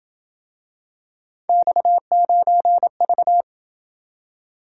Morse code, heard as a keyed tone, about 28 wpm, 700 Hz. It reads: X9V